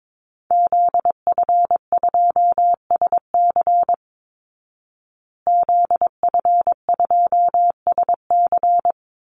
Morse code, heard as a keyed tone, about 22 wpm, 700 Hz. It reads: ZF2SC ZF2SC
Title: Call Signs Repeated 3x - 22wpm